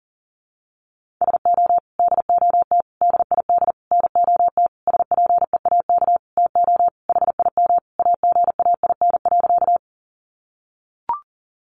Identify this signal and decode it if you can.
Morse code, heard as a keyed tone, about 40 wpm, 700 Hz. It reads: SO DOT DID NOT SPEAK TO HIM AGAIN.
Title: Sentences from Top 1500 Words 40wpm